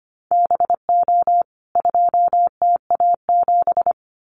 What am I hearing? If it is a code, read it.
Morse code, heard as a keyed tone, about 25 wpm, 700 Hz. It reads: BO 2TA7
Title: CWT Exchange 25wpm